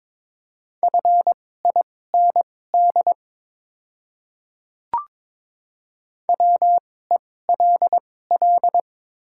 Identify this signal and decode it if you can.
Morse code, heard as a keyed tone, about 22 wpm, spaced as for 16 wpm, 700 Hz. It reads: FIND WELL
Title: 2x Character Spacing - Top 300 Words - 22wpm